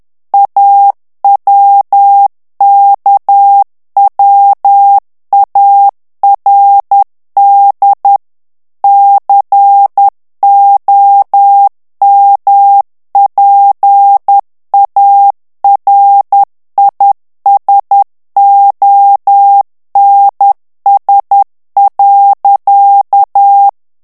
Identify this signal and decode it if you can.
Morse code, heard as a keyed tone, about 10 wpm, 800 Hz. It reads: AWKWARD COMPARISONS.
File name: morse mp3 / ANGLAIS texte clair 2 vitesse 10 mots minute